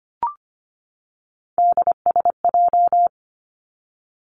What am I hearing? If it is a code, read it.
Morse code, heard as a keyed tone, about 25 wpm, 700 Hz. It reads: DSJ
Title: Three Challenging Letter-Number Combinations 25wpm